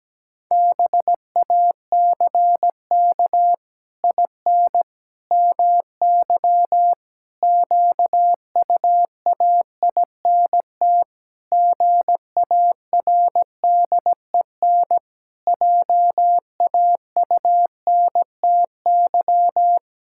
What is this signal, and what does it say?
Morse code, heard as a keyed tone, about 17 wpm, 700 Hz. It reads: BACK IN MY QUAINT GARDEN JAUNTY
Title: Pangram Challenge v2 - 17wpm